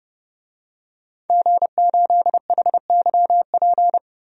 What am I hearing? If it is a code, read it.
Morse code, heard as a keyed tone, about 32 wpm, 700 Hz. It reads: G8HYP
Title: DXCC-100 Call Signs - 5 Characters - 30wpm